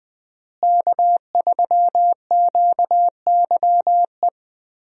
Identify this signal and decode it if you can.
Morse code, heard as a keyed tone, about 21 wpm, 700 Hz. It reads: K3QYE
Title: USA Call Signs - 5 Characters - Repeated 3x - 20wpm